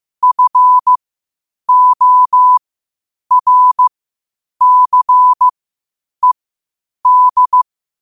Morse code, heard as a keyed tone, about 15 wpm, spaced as for 8 wpm, 1000 Hz. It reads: FORCED
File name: qotdfiles / 8wpm